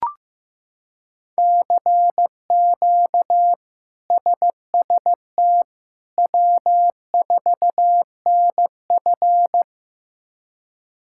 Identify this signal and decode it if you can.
Morse code, heard as a keyed tone, about 15 wpm, 700 Hz. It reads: CQ SST W4NF
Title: K1USN SST QSO Practice - 15wpm